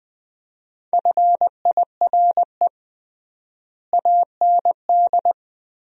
Morse code, heard as a keyed tone, about 20 wpm, 700 Hz. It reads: FIRE AND
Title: 3x Word-Spacing - Sentences from Top 400 Words - 20wpm